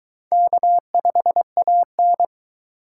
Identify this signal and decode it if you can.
Morse code, heard as a keyed tone, about 22 wpm, 700 Hz. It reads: K5AN